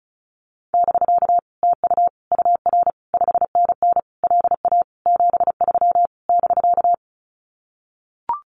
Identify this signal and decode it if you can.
Morse code, heard as a keyed tone, about 34 wpm, 700 Hz. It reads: <BK> TU UR 5NN LA 73 <BK>
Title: Common QSO Phrases Encore 35wpm